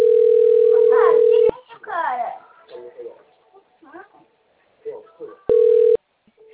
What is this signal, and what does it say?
Ringback tone (North America), 440 + 480 Hz, with speech.
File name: rg-1600-unknown- - - .56077